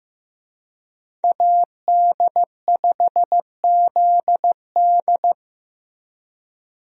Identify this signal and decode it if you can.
Morse code, heard as a keyed tone, about 15 wpm, 700 Hz. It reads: AD5ZD
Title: USA Call Signs - 5 Characters - Repeated 3x - 15wpm